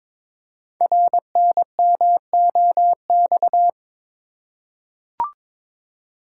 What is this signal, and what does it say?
Morse code, heard as a keyed tone, about 22 wpm, 700 Hz. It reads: RNMOX
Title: Code Groups - 5 Letters - 22wpm